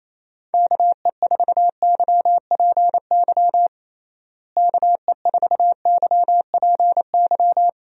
Morse code, heard as a keyed tone, about 28 wpm, 700 Hz. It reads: KE4YPY KE4YPY
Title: USA Call Signs - 6 Characters - Repeated 3x - 28wpm